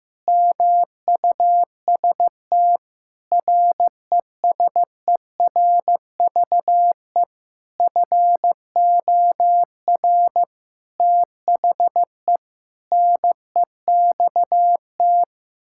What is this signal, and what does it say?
Morse code, heard as a keyed tone, about 15 wpm, 700 Hz. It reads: MUST RESERVE FOR THE NEXT
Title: Sentences from Top 1000 Words - Review Part 6 - 15wpm